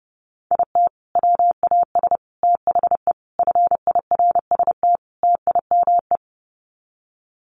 Morse code, heard as a keyed tone, about 30 wpm, 700 Hz. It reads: IT WAS THE FIRST TIME
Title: Sets of 5 Words 30wpm